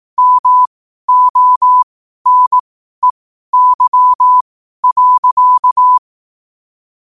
Morse code, heard as a keyed tone, about 18 wpm, spaced as for 12 wpm, 1000 Hz. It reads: MONEY.
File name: qotdfiles / 12wpm